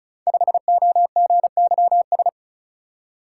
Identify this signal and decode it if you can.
Morse code, heard as a keyed tone, about 36 wpm, 700 Hz. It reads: 5OGYS
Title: Code Groups - 5 Letter:Numbers - 35wpm